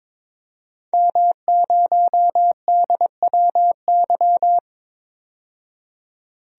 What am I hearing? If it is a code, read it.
Morse code, heard as a keyed tone, about 21 wpm, 700 Hz. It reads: M0DWY
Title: DXCC-100 Call Signs - 5 Characters - 22wpm